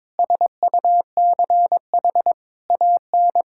Morse code, heard as a keyed tone, about 22 wpm, 700 Hz. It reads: SUCH AN
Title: Sentences from Top 1600 Words 22wpm